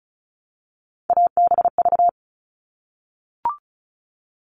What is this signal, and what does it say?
Morse code, heard as a keyed tone, about 34 wpm, 700 Hz. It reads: ABV
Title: All Cut Numbers - 3-Digit 35wpm